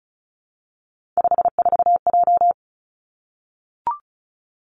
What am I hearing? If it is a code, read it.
Morse code, heard as a keyed tone, about 36 wpm, 700 Hz. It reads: 54J
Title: Three Challenging Letter-Number Combinations 35wpm